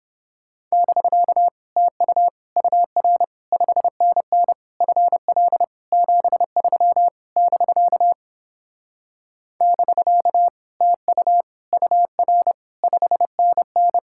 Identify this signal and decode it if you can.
Morse code, heard as a keyed tone, about 31 wpm, 700 Hz. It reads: <BK> TU UR 5NN FL 73 <BK> <BK> TU UR 5NN